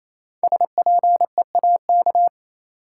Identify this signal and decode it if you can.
Morse code, heard as a keyed tone, about 28 wpm, 700 Hz. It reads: SPEAK
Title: Sentences from Top 1000 Words - Review Part 2 - New Word Drill - 28wpm